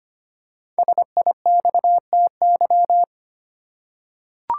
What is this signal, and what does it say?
Morse code, heard as a keyed tone, about 25 wpm, 700 Hz. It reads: SIXTY
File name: 5 Letter Words - First 3 Letters Given - 25wpm